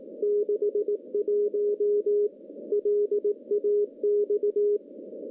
Morse code, heard as a keyed tone, about 18 wpm, 410 Hz. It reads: 61 LAX
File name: CW-SS-k5tr